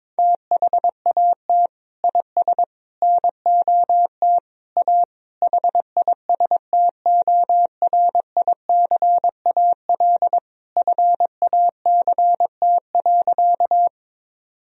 Morse code, heard as a keyed tone, about 22 wpm, 700 Hz. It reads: THAT IS NOT A HISTORICAL FACT.
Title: Sentences from Top 1000 Words - Review Part 4 - 22wpm